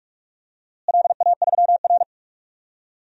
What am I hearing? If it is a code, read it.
Morse code, heard as a keyed tone, about 46 wpm, 700 Hz. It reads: LA3R
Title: DXCC-100 Call Signs - 4 Characters - Repeated 3x - 45wpm